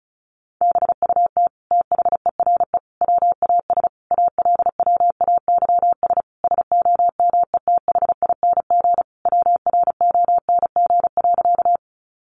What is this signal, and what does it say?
Morse code, heard as a keyed tone, about 34 wpm, 700 Hz. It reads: BUT THERE WAS ALWAYS SOMETHING WRONG.
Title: Sentences from Top 900 Words 35wpm